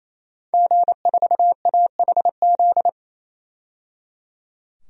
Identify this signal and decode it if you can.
Morse code, heard as a keyed tone, about 27 wpm, 700 Hz. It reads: G4AHZ
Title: DXCC-100 Call Signs - 5 Characters - 28wpm